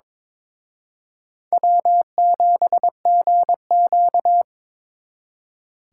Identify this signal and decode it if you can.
Morse code, heard as a keyed tone, about 26 wpm, 700 Hz. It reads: W7GQ
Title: Speed Racing - USA Call Signs 22wpm